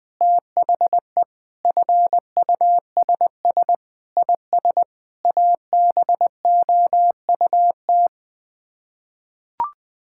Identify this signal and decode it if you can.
Morse code, heard as a keyed tone, about 20 wpm, 700 Hz. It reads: THE FUSS IS ABOUT
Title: Sets of 8 Words - 20wpm